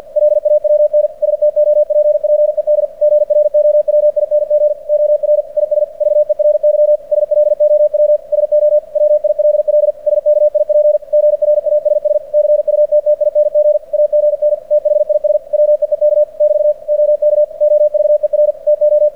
Morse code, heard as a keyed tone, about 48 wpm, 600 Hz. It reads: LAWN STOPPED GROWING OR IS VERY SLOW SO WILL SKIP GRASS CUTTING AGN THIS WEEK <BT> OK OVER TO
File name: archived / CRSnet